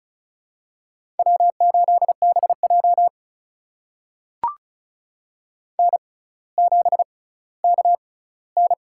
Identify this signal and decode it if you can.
Morse code, heard as a keyed tone, about 36 wpm, spaced as for 18 wpm, 700 Hz. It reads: W8BJ N7KN
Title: USA Call Signs - 4 Characters - 6x Character Spacing - 35wpm